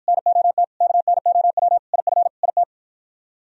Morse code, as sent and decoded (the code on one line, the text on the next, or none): -. --- - / -.- -. --- .-- / .. ..-. / .. -
NOT KNOW IF IT